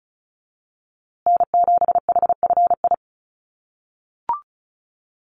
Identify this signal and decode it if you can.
Morse code, heard as a keyed tone, about 34 wpm, 700 Hz. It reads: N7HFI
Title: USA Call Signs Repeated 2x 35wpm